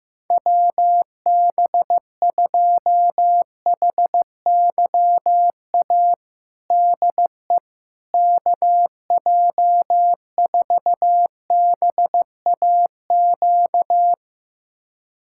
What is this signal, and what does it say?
Morse code, heard as a keyed tone, about 15 wpm, 700 Hz. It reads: WB2HYA DE KJ4BAQ
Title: USA Callsign DE Callsign - 6 Characters - 15wpm